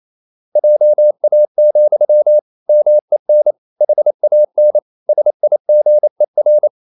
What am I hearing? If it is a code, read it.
Morse code, heard as a keyed tone, about 28 wpm, 600 Hz. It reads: JA, MEN HAN SIGER
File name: Ferie QRQ 28wpm